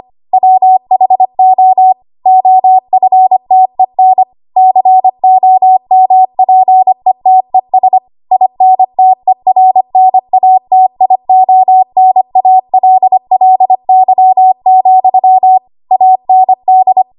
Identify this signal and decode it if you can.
Morse code, heard as a keyed tone, about 25 wpm, 750 Hz. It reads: WHO OFTEN COMPETES INTERNATIONALLY, AND
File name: Archive / 25 WPM / 25WPM